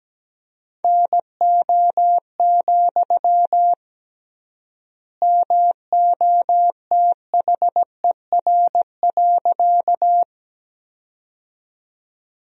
Morse code, heard as a keyed tone, about 17 wpm, 700 Hz. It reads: NO, MOTHER.
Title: 3x Word-Spacing - Sentences from Top 200 Words - 17wpm